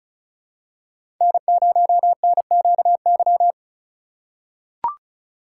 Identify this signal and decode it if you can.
Morse code, heard as a keyed tone, about 34 wpm, 700 Hz. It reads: N0NQY
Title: USA Call Signs - 5 Characters - 6x Character Spacing - 35wpm